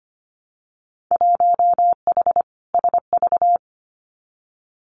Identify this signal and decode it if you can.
Morse code, heard as a keyed tone, about 25 wpm, 700 Hz. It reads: 1H SV
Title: Winter Field Day Exchanges 25wpm